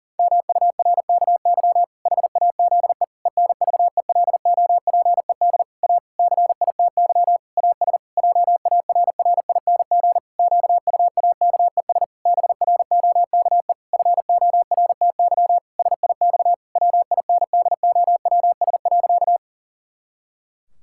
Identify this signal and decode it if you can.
Morse code, heard as a keyed tone, about 37 wpm, 700 Hz. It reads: MURKY HAZE ENVELOPED A CITY AS JARRING QUAKES BROKE FORTY SIX WINDOWS.
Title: Pangram Challenge v2 - 40wpm